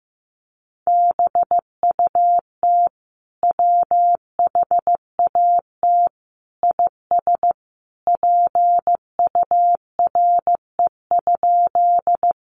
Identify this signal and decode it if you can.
Morse code, heard as a keyed tone, about 15 wpm, 700 Hz. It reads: BUT WHAT IS PURE?
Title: Sentences from Top 1500 Words 15wpm